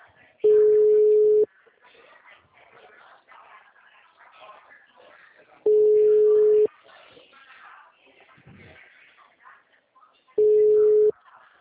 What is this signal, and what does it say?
Ringback tone (Europe, 425 Hz), with speech.